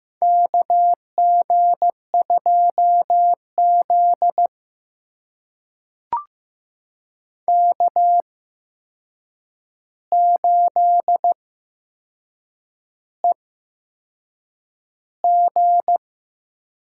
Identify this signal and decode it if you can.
Morse code, heard as a keyed tone, about 15 wpm, spaced as for 6 wpm, 700 Hz. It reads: KG2Z K8EG
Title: USA Call Signs - 4 Characters - 8x Character Spacing - 15wpm